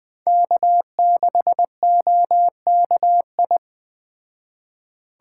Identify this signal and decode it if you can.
Morse code, heard as a keyed tone, about 20 wpm, 700 Hz. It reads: K6OKI
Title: Speed Racing - USA 1x3 Call Signs 17wpm